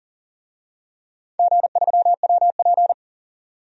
Morse code, heard as a keyed tone, about 39 wpm, 700 Hz. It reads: G3WP
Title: DXCC-100 Call Signs - 4 Characters - 40wpm